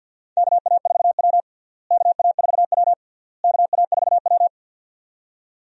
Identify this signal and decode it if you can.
Morse code, heard as a keyed tone, about 48 wpm, 700 Hz. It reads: KA4W KA4W KA4W
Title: USA Call Signs - 4 Characters - Repeated 3x - 50wpm